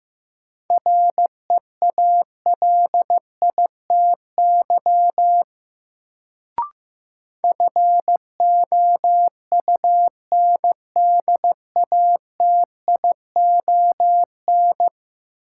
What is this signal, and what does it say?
Morse code, heard as a keyed tone, about 15 wpm, 700 Hz. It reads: REALITY FOUNDATION
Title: Sentences from Top 1200 Words - New Word Drill - 15wpm